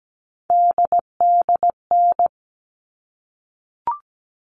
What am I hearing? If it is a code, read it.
Morse code, heard as a keyed tone, about 17 wpm, 700 Hz. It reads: DDN
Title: All Cut Numbers - 3-Digit 17wpm